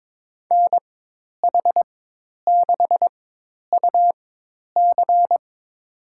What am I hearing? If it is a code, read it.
Morse code, heard as a keyed tone, about 22 wpm, spaced as for 10 wpm, 700 Hz. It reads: NH6UC
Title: USA Call Signs - 5 Characters - 4x Character Spacing - 22wpm